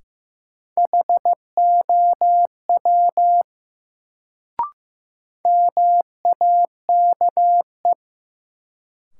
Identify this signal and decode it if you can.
Morse code, heard as a keyed tone, about 16 wpm, 700 Hz. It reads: HOW MAKE
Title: Sentences from Top 100 Words - New Word Drill - 15wpm